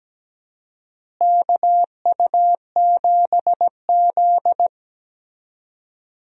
Morse code, heard as a keyed tone, about 17 wpm, 700 Hz. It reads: KU7Z
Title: USA Call Signs - 4 Characters - Repeated 3x - 17wpm